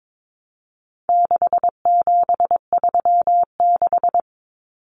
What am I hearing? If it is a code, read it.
Morse code, heard as a keyed tone, about 22 wpm, 700 Hz. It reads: 6736
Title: Common Cut Numbers - 4-Digit 22wpm